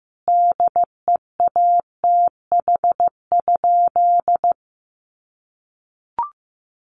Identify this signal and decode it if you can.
Morse code, heard as a keyed tone, about 14 wpm, 700 Hz. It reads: DEATH?
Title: Sentences from Top 1100 Words 15wpm